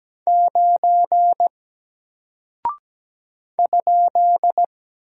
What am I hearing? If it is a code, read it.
Morse code, heard as a keyed tone, about 17 wpm, 700 Hz. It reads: 9 ?
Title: Single Character (Letter:Number:Punctuation:Prosign) Alt 17wpm